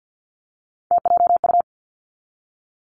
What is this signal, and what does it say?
Morse code, heard as a keyed tone, about 50 wpm, 700 Hz. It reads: TJU
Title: Three Challenging Letter-Number Combinations 50wpm